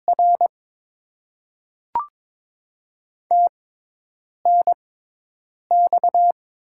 Morse code, heard as a keyed tone, about 22 wpm, spaced as for 8 wpm, 690 Hz. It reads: R TNX
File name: Top 20 QSO Elements - 6x Character Spacing - 22wpm